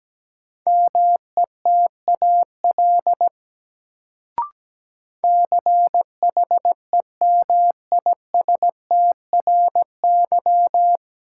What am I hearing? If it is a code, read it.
Morse code, heard as a keyed tone, about 17 wpm, 700 Hz. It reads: METAL CHEMISTRY
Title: Sentences from Top 1000 Words - Review Part 6 - New Word Drill - 17wpm